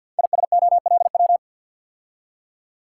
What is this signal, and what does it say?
Morse code, heard as a keyed tone, about 47 wpm, 700 Hz. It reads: ISOLW
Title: Code Groups - 5 Letter:Numbers - 50wpm